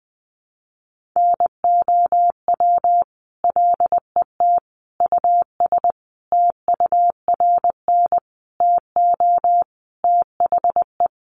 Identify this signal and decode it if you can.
Morse code, heard as a keyed tone, about 20 wpm, 700 Hz. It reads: NOW LET US TURN TO THE
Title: Sentences from Top 1400 Words 20wpm